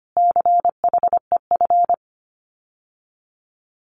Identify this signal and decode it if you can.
Morse code, heard as a keyed tone, about 25 wpm, 700 Hz. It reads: CHEF
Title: Common QSO Phrases Encore 25wpm